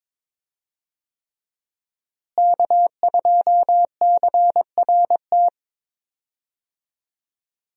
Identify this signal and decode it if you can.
Morse code, heard as a keyed tone, about 22 wpm, 700 Hz. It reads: K2CRT
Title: USA Call Signs - 5 Characters - Repeated 3x - 22wpm